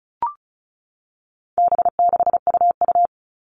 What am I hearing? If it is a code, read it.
Morse code, heard as a keyed tone, about 34 wpm, 700 Hz. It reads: B6UU
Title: All Cut Numbers - 4-Digit 35wpm